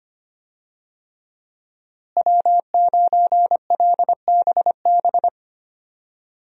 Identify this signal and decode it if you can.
Morse code, heard as a keyed tone, about 25 wpm, 700 Hz. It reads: W9LBB
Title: USA Call Signs - 5 Characters - Repeated 3x - 25wpm